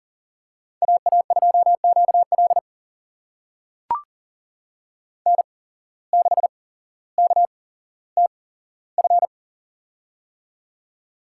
Morse code, heard as a keyed tone, about 40 wpm, spaced as for 17 wpm, 700 Hz. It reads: AA2QL N6KTF
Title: USA Call Signs - 5 Characters - 8x Character Spacing - 40wpm